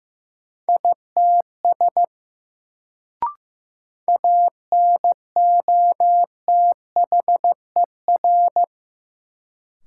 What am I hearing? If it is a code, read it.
Morse code, heard as a keyed tone, about 15 wpm, 700 Hz. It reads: ITS ANOTHER
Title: Sentences from Top 600 Words - New Word Drill - 15wpm